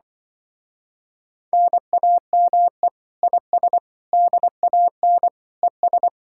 Morse code, heard as a keyed tone, about 24 wpm, 700 Hz. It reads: NAME IS DAN ES
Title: Speed-Racing Common QSO Phrases Encore - 20wpm